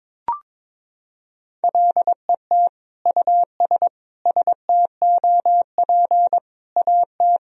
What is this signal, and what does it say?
Morse code, heard as a keyed tone, about 22 wpm, 700 Hz. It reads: LET US STOP AT
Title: Sentences from Top 400 Words and Abbreviations 22wpm